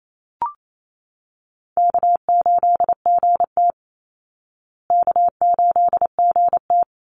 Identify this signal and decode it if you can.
Morse code, heard as a keyed tone, about 28 wpm, 700 Hz. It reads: K8GT K8GT
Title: Easy Call Signs Repeated 2x - 28wpm